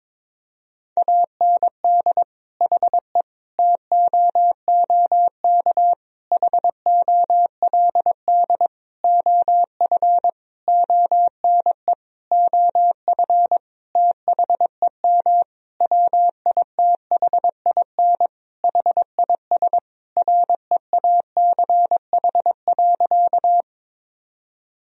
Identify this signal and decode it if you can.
Morse code, heard as a keyed tone, about 22 wpm, 700 Hz. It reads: AND HE TOOK HOLD OF ONE OF THEM WITHIN HIS REACH.
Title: Sentences from Top 1000 Words - Review Part 4 - 22wpm